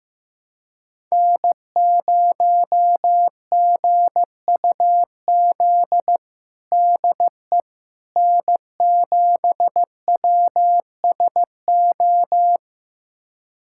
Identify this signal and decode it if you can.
Morse code, heard as a keyed tone, about 15 wpm, 700 Hz. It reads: N0GUZ DE N7WSO
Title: USA Callsign DE Callsign - 5 Characters - 15wpm